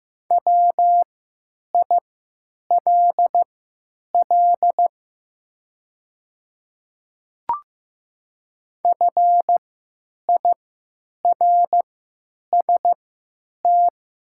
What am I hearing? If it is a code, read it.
Morse code, heard as a keyed tone, about 15 wpm, spaced as for 9 wpm, 690 Hz. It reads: WILL FIRST
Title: 3x Character Spacing - Top 100 Words - 15wpm